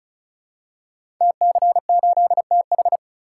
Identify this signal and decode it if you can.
Morse code, heard as a keyed tone, about 36 wpm, 700 Hz. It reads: TC8TH
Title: Code Groups - 5 Letter:Numbers - 35wpm